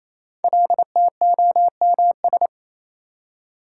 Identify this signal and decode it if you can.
Morse code, heard as a keyed tone, about 28 wpm, 700 Hz. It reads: LTOMS
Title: Code Groups - 5 Letter:Numbers - 28wpm